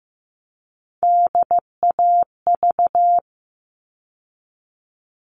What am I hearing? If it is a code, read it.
Morse code, heard as a keyed tone, about 15 wpm, 700 Hz. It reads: DAV
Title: All Cut Numbers - 3-Digit 15wpm